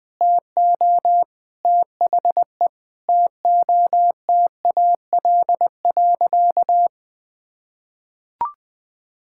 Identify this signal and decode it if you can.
Morse code, heard as a keyed tone, about 20 wpm, 700 Hz. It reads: TO THE TOTAL.
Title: Sentences with 8-Word Phrases - 20wpm